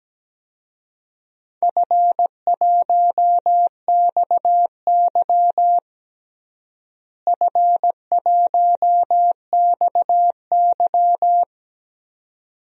Morse code, heard as a keyed tone, about 17 wpm, 700 Hz. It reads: F1XY F1XY
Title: DXCC-100 Call Signs - 4 Characters - Repeated 2x - 17wpm